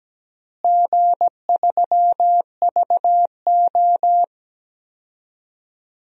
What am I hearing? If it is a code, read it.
Morse code, heard as a keyed tone, about 17 wpm, 700 Hz. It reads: G3VO
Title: DXCC-100 Call Signs - 4 Characters - Repeated 3x - 17wpm